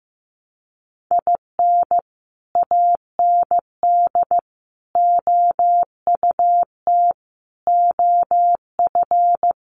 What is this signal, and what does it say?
Morse code, heard as a keyed tone, about 15 wpm, 700 Hz. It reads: IN AND OUT OF
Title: Sets of 5 Words 15wpm